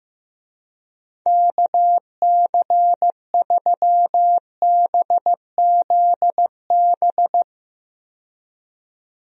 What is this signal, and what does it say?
Morse code, heard as a keyed tone, about 15 wpm, 700 Hz. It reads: KC3BZB
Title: USA Call Signs - 6 Characters - 15wpm